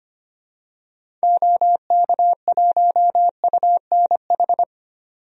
Morse code, heard as a keyed tone, about 25 wpm, 700 Hz. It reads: OK1UNH
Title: DXCC-100 Call Signs - 6 Characters - 25wpm